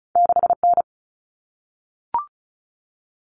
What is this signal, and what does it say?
Morse code, heard as a keyed tone, about 36 wpm, 700 Hz. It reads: 6N
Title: Common Cut Numbers - 2-Digit 35wpm